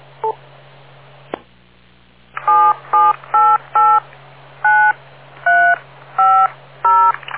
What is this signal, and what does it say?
DTMF, heard as telephone touch tones, tones of about 280 ms. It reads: 77889350